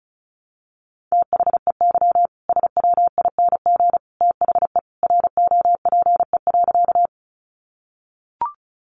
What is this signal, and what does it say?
Morse code, heard as a keyed tone, about 34 wpm, 700 Hz. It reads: THEY SWING THE ROPE.
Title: Easy Phrases 35wpm